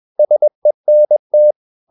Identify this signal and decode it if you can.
Morse code, heard as a keyed tone, about 21 wpm, 600 Hz. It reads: SENT
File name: Ferie QRS 21wpm